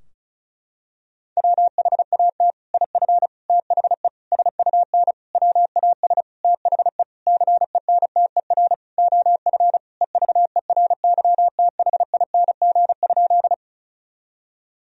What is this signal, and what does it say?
Morse code, heard as a keyed tone, about 34 wpm, 700 Hz. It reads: WHAT IF THE SUN WAS THE CENTER OF EVERYTHING?
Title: Sentences from Top 1000 Words - Review Part 2 - 35wpm